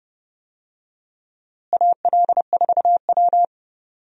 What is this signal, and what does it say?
Morse code, heard as a keyed tone, about 30 wpm, 700 Hz. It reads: AL4W
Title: USA Call Signs - 4 Characters - Repeated 2x - 30wpm